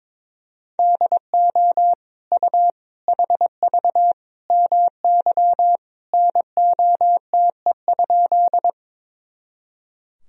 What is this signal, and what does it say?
Morse code, heard as a keyed tone, about 22 wpm, 700 Hz. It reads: DO U HV MY NOTE?
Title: Sentences from Top 500 Words and Abbreviations 22wpm